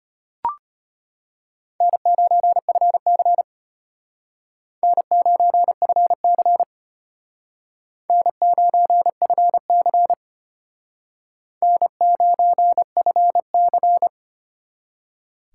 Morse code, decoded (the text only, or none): N9FC N9FC N9FC N9FC